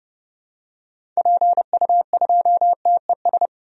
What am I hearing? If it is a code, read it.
Morse code, heard as a keyed tone, about 30 wpm, 700 Hz. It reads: PU2TES
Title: DXCC-100 Call Signs - 6 Characters - 30wpm